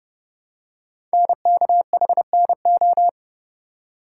Morse code, heard as a keyed tone, about 28 wpm, 700 Hz. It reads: NKHNO
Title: Code Groups - 5 Letters - 30wpm